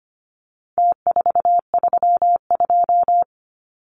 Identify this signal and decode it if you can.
Morse code, heard as a keyed tone, about 25 wpm, 700 Hz. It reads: T432
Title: Numbers - Contest Style - 4-Digit 25wpm